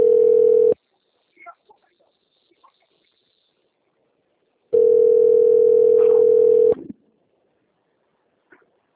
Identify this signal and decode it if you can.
Ringback tone (North America), 440 + 480 Hz.